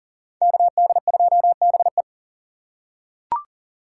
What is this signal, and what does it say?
Morse code, heard as a keyed tone, about 40 wpm, 700 Hz. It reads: KD2BE
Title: USA Call Signs - 5 Characters - 3x Character Spacing - 40wpm